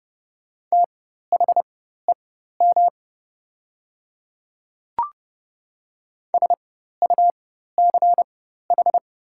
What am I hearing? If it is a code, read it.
Morse code, heard as a keyed tone, about 30 wpm, spaced as for 14 wpm, 700 Hz. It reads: THEM SUCH
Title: 4x Character Spacing - Top 300 Words - 30wpm